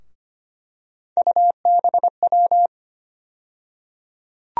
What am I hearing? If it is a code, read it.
Morse code, heard as a keyed tone, about 25 wpm, 700 Hz. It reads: UBW
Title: USA Call Sign Postfixes - 3 Letters 25wpm